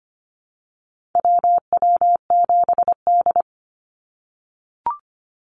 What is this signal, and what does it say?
Morse code, heard as a keyed tone, about 25 wpm, 700 Hz. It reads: WW7D
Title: NA SOTA Activator Callsigns 25wpm